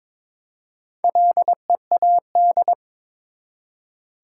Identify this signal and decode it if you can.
Morse code, heard as a keyed tone, about 22 wpm, 700 Hz. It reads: LEAD
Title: Four Letter Words - 22wpm